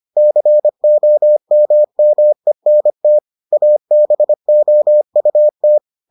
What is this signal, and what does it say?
Morse code, heard as a keyed tone, about 25 wpm, 600 Hz. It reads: COMMENT ABOUT